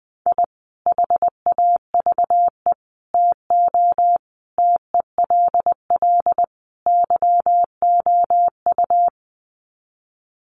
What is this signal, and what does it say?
Morse code, heard as a keyed tone, about 20 wpm, 700 Hz. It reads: I HAVE TO TELL YOU
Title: Sets of 5 Words 20wpm